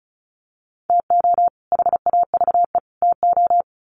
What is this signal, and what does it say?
Morse code, heard as a keyed tone, about 34 wpm, 700 Hz. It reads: TO HAVE TO
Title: Sets of 3 Words 35wpm